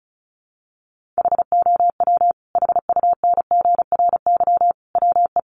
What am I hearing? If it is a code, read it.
Morse code, heard as a keyed tone, about 34 wpm, 700 Hz. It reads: HOW HUNGRY WE
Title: Sentences from Top 1400 Words 35wpm